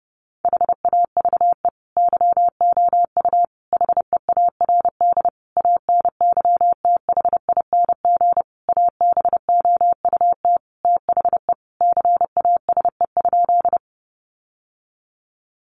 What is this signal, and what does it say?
Morse code, heard as a keyed tone, about 30 wpm, 700 Hz. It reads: HAVE YOU HEARD ANYTHING ABOUT THE CASE?
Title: Sentences from Top 1000 Words 30wpm